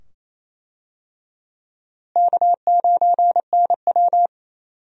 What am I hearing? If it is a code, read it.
Morse code, heard as a keyed tone, about 28 wpm, 700 Hz. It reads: K9NW
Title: USA Call Signs - 4 Characters - Repeated 2x - 28wpm